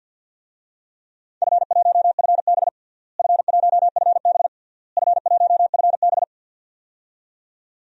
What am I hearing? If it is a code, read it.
Morse code, heard as a keyed tone, about 51 wpm, 700 Hz. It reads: F1FB F1FB F1FB
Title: DXCC-100 Call Signs - 4 Characters - Repeated 3x - 50wpm